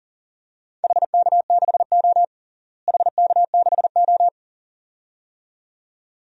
Morse code, heard as a keyed tone, about 40 wpm, 700 Hz. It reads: HK6O HK6O
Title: DXCC-100 Call Signs - 4 Characters - Repeated 2x - 40wpm